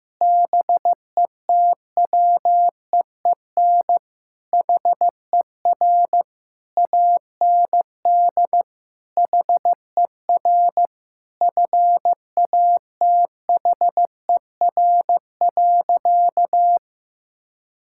Morse code, heard as a keyed tone, about 16 wpm, 700 Hz. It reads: BETWEEN HER AND HER FATHER.
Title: Sentences from Top 1000 Words - Review Part 6 - 15wpm